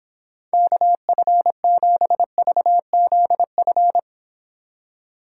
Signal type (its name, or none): Morse code